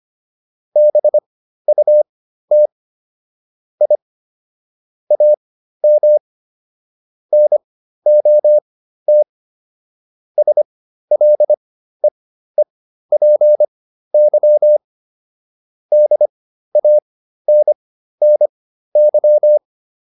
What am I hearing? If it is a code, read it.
Morse code, heard as a keyed tone, about 25 wpm, spaced as for 13 wpm, 600 Hz. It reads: BUT I AM NOT SLEEPY DANNY